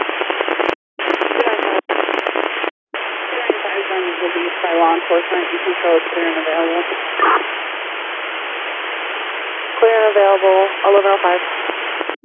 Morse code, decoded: none

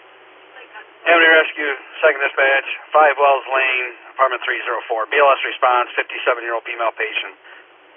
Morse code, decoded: none

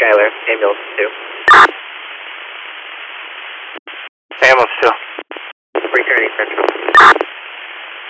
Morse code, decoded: none

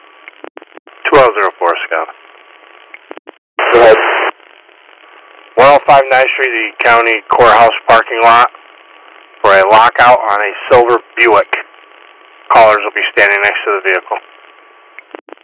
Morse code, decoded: none

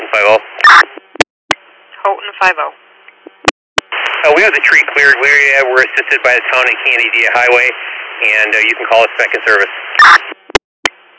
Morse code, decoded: none

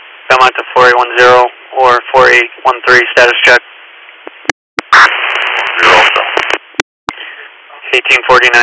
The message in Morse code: none

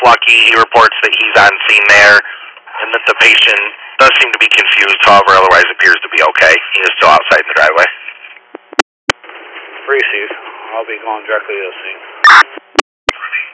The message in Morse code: none

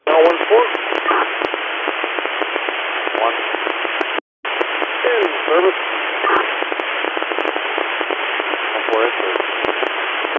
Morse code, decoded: none

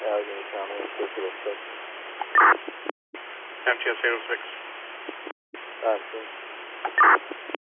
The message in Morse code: none

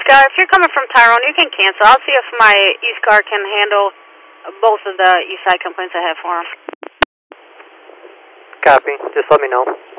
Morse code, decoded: none

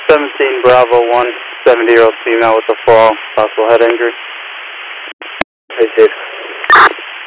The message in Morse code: none

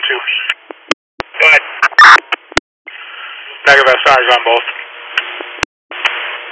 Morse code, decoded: none

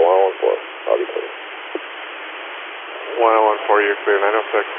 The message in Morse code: none

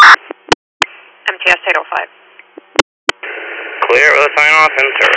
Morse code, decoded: none

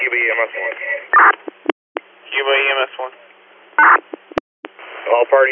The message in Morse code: none

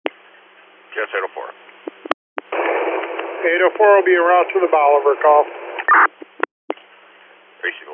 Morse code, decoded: none

